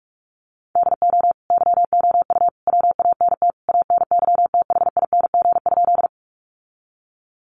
Morse code, heard as a keyed tone, about 44 wpm, 700 Hz. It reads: DO YOU WANT ANYTHING?
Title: Sentences from Top 1000 Words 45wpm